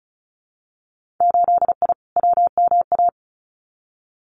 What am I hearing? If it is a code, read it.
Morse code, heard as a keyed tone, about 36 wpm, 700 Hz. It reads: 8I WMA
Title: Winter Field Day Exchanges Encore 35wpm